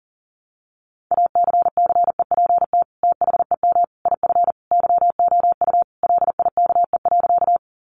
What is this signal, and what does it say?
Morse code, heard as a keyed tone, about 40 wpm, 700 Hz. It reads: ACCEPT THEM IF YOU LIKE.
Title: Sentences from Top 1300 Words 40wpm